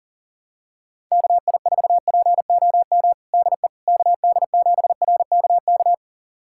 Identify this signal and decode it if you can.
Morse code, heard as a keyed tone, about 40 wpm, 700 Hz. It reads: KI4POM DE KD7RKK
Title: USA Callsign DE Callsign - 6 Characters - 40wpm